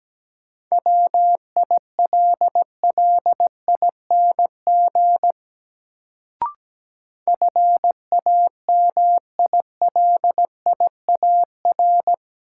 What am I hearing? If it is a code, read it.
Morse code, heard as a keyed tone, about 17 wpm, 700 Hz. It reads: WILLING FAMILIAR
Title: Sentences from Top 1000 Words - New Word Drill - 17wpm